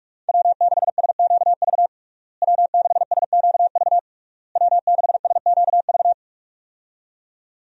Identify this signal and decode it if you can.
Morse code, heard as a keyed tone, about 46 wpm, 700 Hz. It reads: W6SQV W6SQV W6SQV
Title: USA Call Signs - 5 Characters - Repeated 3x - 45wpm